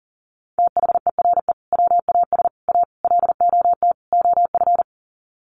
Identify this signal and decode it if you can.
Morse code, heard as a keyed tone, about 40 wpm, 700 Hz. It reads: THERE WAS A LOT OF
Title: Sets of 5 Words 40wpm